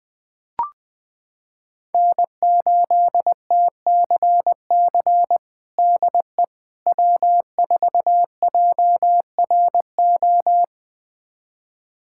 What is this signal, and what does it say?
Morse code, heard as a keyed tone, about 21 wpm, 700 Hz. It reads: N8TCC DE W4JRO
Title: USA Callsign DE Callsign - 5 Characters - 20wpm